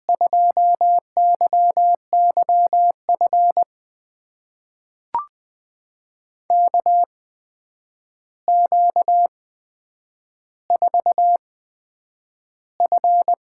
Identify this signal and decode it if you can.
Morse code, heard as a keyed tone, about 20 wpm, spaced as for 9 wpm, 700 Hz. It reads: 2YYF KQ4F
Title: USA Call Signs - 5 Characters - 8x Character Spacing - 20wpm